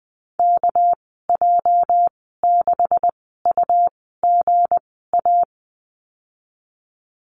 Morse code, heard as a keyed tone, about 20 wpm, spaced as for 14 wpm, 700 Hz. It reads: KJ6UGA
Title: 219 Lesson 35g - Number 0 - Call Signs 20wpm